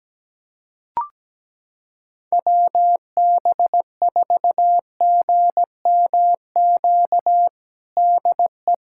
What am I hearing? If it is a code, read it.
Morse code, heard as a keyed tone, about 17 wpm, 700 Hz. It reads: WB4GMQ DE
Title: USA Callsign DE Callsign - 6 Characters - 17wpm